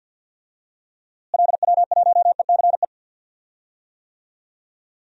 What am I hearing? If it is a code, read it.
Morse code, heard as a keyed tone, about 52 wpm, 700 Hz. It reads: LW1ECE
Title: DXCC-100 Call Signs - 6 Characters - Repeated 3x - 50wpm